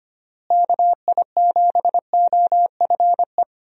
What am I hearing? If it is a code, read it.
Morse code, heard as a keyed tone, about 25 wpm, 700 Hz. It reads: KI7OFE
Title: USA Call Signs - 6 Characters - Repeated 3x - 25wpm